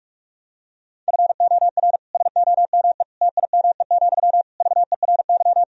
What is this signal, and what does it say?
Morse code, heard as a keyed tone, about 46 wpm, 700 Hz. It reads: FOR SOME TIME, VERY
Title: Sentences from Top 1000 Words - Review Part 4 - 45wpm